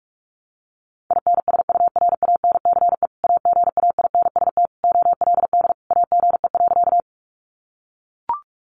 Morse code, heard as a keyed tone, about 44 wpm, 700 Hz. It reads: INSURANCE AGAINST OLD AGE.
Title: Sentences from Top 1100 Words 45wpm